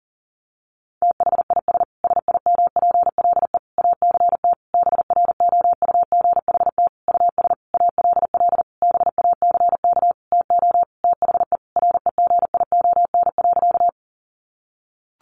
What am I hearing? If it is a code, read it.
Morse code, heard as a keyed tone, about 37 wpm, 700 Hz. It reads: THIS SIMPLE ACT BROUGHT US ALL BACK TO THE REGION.
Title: Sentences from Top 1500 Words 40wpm